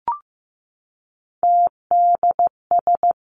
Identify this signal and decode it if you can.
Morse code, heard as a keyed tone, about 15 wpm, 700 Hz. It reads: TDS
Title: Three Challenging Letter-Number Combinations 15wpm